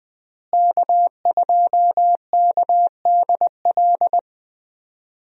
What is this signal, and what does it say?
Morse code, heard as a keyed tone, about 21 wpm, 700 Hz. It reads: K2KDL
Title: USA Call Signs - 5 Characters - Repeated 2x - 20wpm